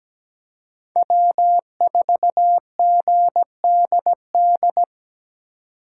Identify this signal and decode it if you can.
Morse code, heard as a keyed tone, about 17 wpm, 700 Hz. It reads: W4GDD
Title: USA Call Signs - 4x Character Spacing - 17wpm